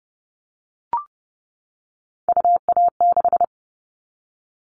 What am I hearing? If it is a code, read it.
Morse code, heard as a keyed tone, about 30 wpm, 700 Hz. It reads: UA6
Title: Callsign Prefixes for Top 100 DXCC - 30wpm